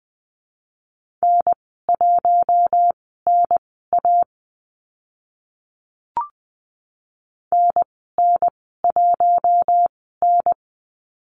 Morse code, heard as a keyed tone, about 20 wpm, spaced as for 15 wpm, 700 Hz. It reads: N1NA NN1N
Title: 042 Lesson 8g - Number 1 - Call Signs 20wpm